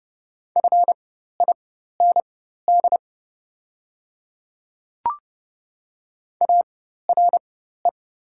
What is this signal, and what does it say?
Morse code, heard as a keyed tone, about 29 wpm, spaced as for 14 wpm, 700 Hz. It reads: FIND ARE
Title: 4x Character Spacing - Top 200 Words - 30wpm